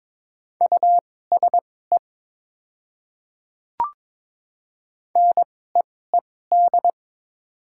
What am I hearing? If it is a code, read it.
Morse code, heard as a keyed tone, about 22 wpm, spaced as for 16 wpm, 700 Hz. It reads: USE NEED
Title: 2x Character Spacing - Top 200 Words - 22wpm